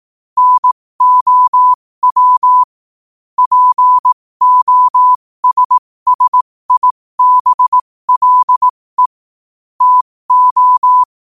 Morse code, heard as a keyed tone, about 18 wpm, spaced as for 15 wpm, 1000 Hz. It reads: NOW POSSIBLE TO
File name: qotdfiles / 15wpm